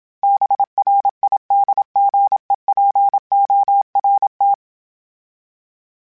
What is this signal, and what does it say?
Morse code, heard as a keyed tone, about 27 wpm, 800 Hz. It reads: BRIDGEPORT
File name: fromSMs / FromSMs mp3s / Cities@25